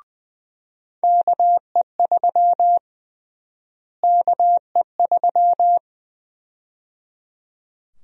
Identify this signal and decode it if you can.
Morse code, heard as a keyed tone, about 20 wpm, 700 Hz. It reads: KE3 KE3
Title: USA Call Sign Prefixes - Repeated 2x - 20wpm